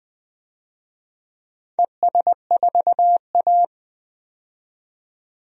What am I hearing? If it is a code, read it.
Morse code, heard as a keyed tone, about 21 wpm, 700 Hz. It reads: ES4A
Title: DXCC-100 Call Signs - 4 Characters - 20wpm